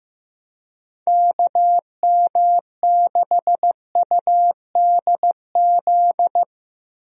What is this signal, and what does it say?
Morse code, heard as a keyed tone, about 15 wpm, 700 Hz. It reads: KM6UDZ
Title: USA Call Signs - 6 Characters - 15wpm